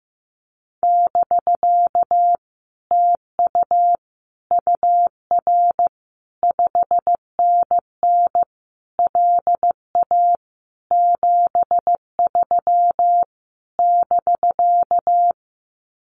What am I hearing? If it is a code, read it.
Morse code, heard as a keyed tone, about 15 wpm, 700 Hz. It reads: <BK> TU UR 5NN LA 73 <BK>
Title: Common QSO Phrases Encore 15wpm